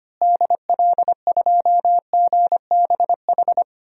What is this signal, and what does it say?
Morse code, heard as a keyed tone, about 25 wpm, 700 Hz. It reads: DL2GBH
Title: DXCC-100 Call Signs - 6 Characters - Repeated 3x - 25wpm